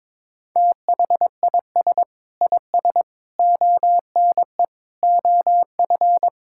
Morse code, heard as a keyed tone, about 22 wpm, 700 Hz. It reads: THIS IS ONE OF
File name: practice / 22wpm / Sentences with 4-Word Phrases - 22wpm